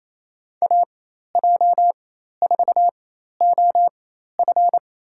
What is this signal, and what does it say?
Morse code, heard as a keyed tone, about 28 wpm, spaced as for 13 wpm, 700 Hz. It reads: AJ4OF
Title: USA Call Signs - 5 Characters - 4x Character Spacing - 28wpm